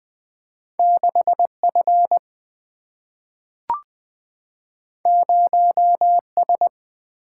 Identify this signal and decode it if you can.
Morse code, heard as a keyed tone, about 20 wpm, 700 Hz. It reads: 6 F 0 S
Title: Single-Letter Single-Number 20wpm